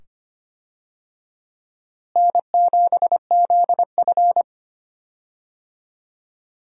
Morse code, heard as a keyed tone, about 25 wpm, 700 Hz. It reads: N7ZF
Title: USA Call Signs - 4 Characters - Repeated 2x - 25wpm